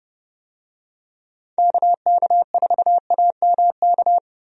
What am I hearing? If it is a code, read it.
Morse code, heard as a keyed tone, about 31 wpm, 700 Hz. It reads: KK4AMK
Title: USA Call Signs - 6 Characters - Repeated 3x - 30wpm